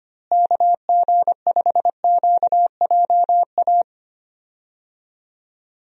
Morse code, heard as a keyed tone, about 25 wpm, 700 Hz. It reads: KG5QJA
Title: USA Call Signs - 6 Characters - Repeated 3x - 25wpm